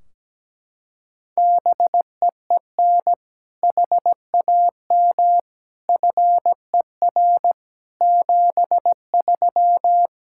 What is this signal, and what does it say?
Morse code, heard as a keyed tone, about 17 wpm, 700 Hz. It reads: BEEN HAM FER 73
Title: Practice QSO - Exchange 3 - More Info - 17wpm